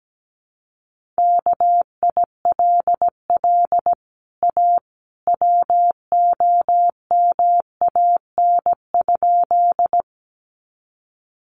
Morse code, heard as a keyed tone, about 17 wpm, 700 Hz. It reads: KILL A WOMAN?
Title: Sentences from Top 1400 Words 17wpm